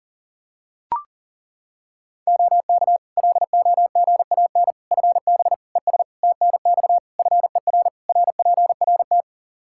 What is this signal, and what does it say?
Morse code, heard as a keyed tone, about 40 wpm, 700 Hz. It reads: OK LOGAN FB ES TNX FER RPRT
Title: Practice QSO - Exchange 2 - Info - 40wpm